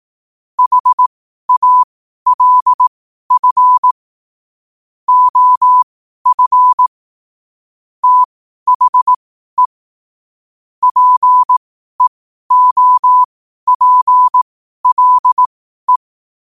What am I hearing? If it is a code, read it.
Morse code, heard as a keyed tone, about 18 wpm, spaced as for 12 wpm, 1000 Hz. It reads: HALF OF THE PEOPLE
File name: qotdfiles / 12wpm